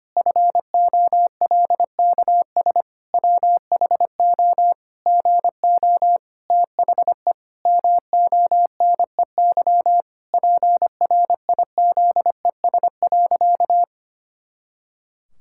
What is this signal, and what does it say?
Morse code, heard as a keyed tone, about 25 wpm, 700 Hz. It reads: FOLKS WHO GO THE MONEY PRIZES.
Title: Pangram Challenge v2 - 25wpm